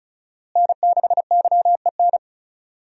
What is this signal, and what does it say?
Morse code, heard as a keyed tone, about 34 wpm, 700 Hz. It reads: N6YEN
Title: USA Call Signs - 5 Characters - Repeated 3x - 35wpm